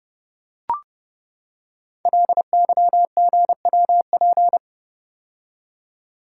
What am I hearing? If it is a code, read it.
Morse code, heard as a keyed tone, about 30 wpm, 700 Hz. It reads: LYGWP
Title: Code Groups - 5 Letter:Numbers - 30wpm